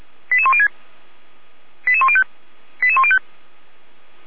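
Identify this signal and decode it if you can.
Five-tone selective call (ZVEI), heard as quick runs of five tones, tones of about 70 ms. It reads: 88106 88105 88105
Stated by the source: ZVEI five-tone selective call 88103 received by SDR, Feuerwehr Orlamünde fire brigade alerting.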